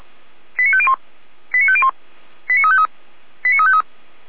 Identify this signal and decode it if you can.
Five-tone selective call (ZVEI), heard as quick runs of five tones, tones of about 70 ms. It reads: 89501 89501 89353 89353
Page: ZVEI